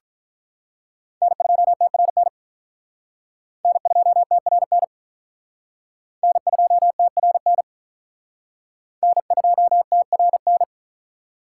Morse code, heard as a keyed tone, about 46 wpm, 700 Hz. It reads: N2TRN N2TRN N2TRN N2TRN